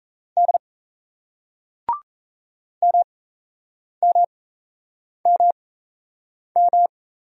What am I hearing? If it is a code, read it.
Morse code, heard as a keyed tone, about 34 wpm, spaced as for 8 wpm, 700 Hz. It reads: D MMMM
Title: Speed-Racing Single Letters 28wpm